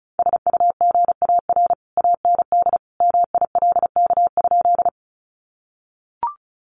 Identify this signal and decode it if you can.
Morse code, heard as a keyed tone, about 34 wpm, 700 Hz. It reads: SUGAR AND MILK?
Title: Sentences from Top 1500 Words 35wpm